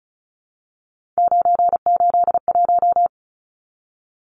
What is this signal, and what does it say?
Morse code, heard as a keyed tone, about 34 wpm, 700 Hz. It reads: 981
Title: Numbers - 3-Digit 35wpm